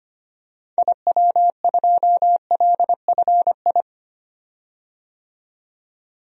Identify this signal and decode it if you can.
Morse code, heard as a keyed tone, about 25 wpm, 700 Hz. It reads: IW2LFI
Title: DXCC-100 Call Signs - 6 Characters - 25wpm